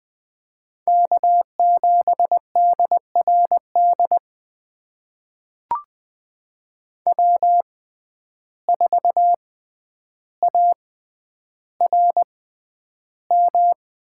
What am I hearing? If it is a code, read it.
Morse code, heard as a keyed tone, about 20 wpm, spaced as for 10 wpm, 700 Hz. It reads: K7DRD W4ARM
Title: USA Call Signs - 5 Characters - 6x Character Spacing - 20wpm